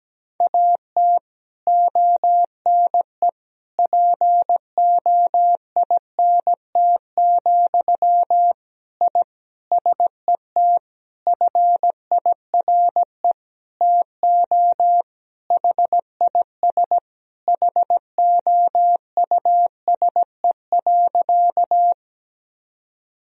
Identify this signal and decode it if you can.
Morse code, heard as a keyed tone, about 17 wpm, 700 Hz. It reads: AT ONE POINT, I SET FIRE TO HIS HOUSE.
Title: Sentences from Top 1000 Words - Review Part 1 - 17wpm